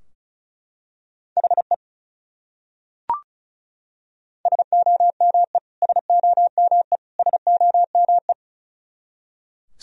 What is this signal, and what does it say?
Morse code, heard as a keyed tone, about 36 wpm, 700 Hz. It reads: HE SOME SOME SOME
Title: Top 100 Words Repeated 35wpm